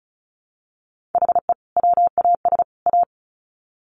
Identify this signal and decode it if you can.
Morse code, heard as a keyed tone, about 34 wpm, 700 Hz. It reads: HE WAS A
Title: Sets of 3 Words 35wpm